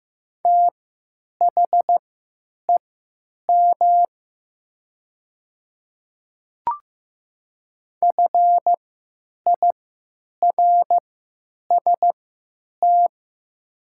Morse code, heard as a keyed tone, about 15 wpm, spaced as for 9 wpm, 700 Hz. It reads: THEM FIRST
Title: 3x Character Spacing - Top 100 Words - 15wpm